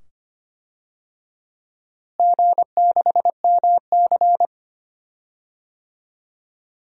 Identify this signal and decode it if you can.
Morse code, heard as a keyed tone, about 25 wpm, 700 Hz. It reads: G6MC
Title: DXCC-100 Call Signs - 4 Characters - Repeated 2x - 25wpm